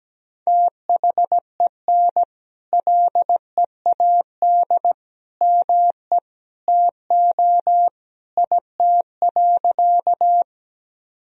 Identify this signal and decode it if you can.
Morse code, heard as a keyed tone, about 17 wpm, 700 Hz. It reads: THEN LEAD ME TO IT.
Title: Sentences from Top 1000 Words - Review Part 2 - 17wpm